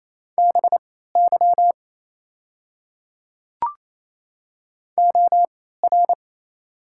Morse code, heard as a keyed tone, about 28 wpm, spaced as for 16 wpm, 700 Hz. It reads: BY OR
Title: 3x Character Spacing - Top 100 Words - 28wpm